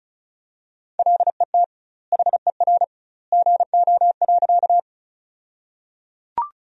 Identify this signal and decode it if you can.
Morse code, heard as a keyed tone, about 34 wpm, 700 Hz. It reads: LET HER GO.
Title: 2x Word-Spacing - Sentences from Top 300 Words - 35wpm